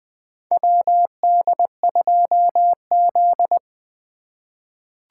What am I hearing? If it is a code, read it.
Morse code, heard as a keyed tone, about 20 wpm, 700 Hz. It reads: WD2Z
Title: USA Call Signs - 2x Character Spacing - 20wpm